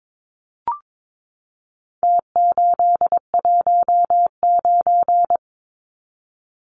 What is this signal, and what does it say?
Morse code, heard as a keyed tone, about 22 wpm, 700 Hz. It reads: T819
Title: Numbers - Contest Style - 4-Digit 22wpm